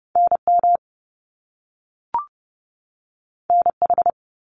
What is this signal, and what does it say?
Morse code, heard as a keyed tone, about 30 wpm, 700 Hz. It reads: NM NH
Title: US States 30wpm